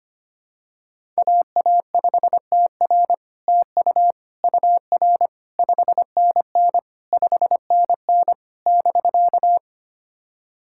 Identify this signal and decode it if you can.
Morse code, heard as a keyed tone, about 25 wpm, 700 Hz. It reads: AA5TR TU UR 5NN 5NN <BK>
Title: SOTA QSO Practice 25wpm